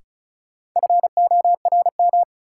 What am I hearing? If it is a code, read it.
Morse code, heard as a keyed tone, about 36 wpm, 700 Hz. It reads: FORM
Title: Sentences from Top 1000 Words - Review Part 1 - New Word Drill - 35wpm